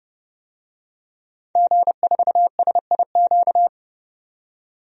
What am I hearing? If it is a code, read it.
Morse code, heard as a keyed tone, about 30 wpm, 700 Hz. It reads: G4SIQ
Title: DXCC-100 Call Signs - 5 Characters - 30wpm